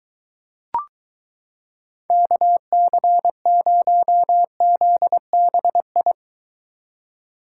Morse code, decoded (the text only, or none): KC0ZBI